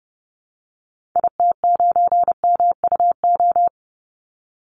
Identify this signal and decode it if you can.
Morse code, heard as a keyed tone, about 30 wpm, 700 Hz. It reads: IT9MUO
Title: Call Signs Repeated 2x - 30wpm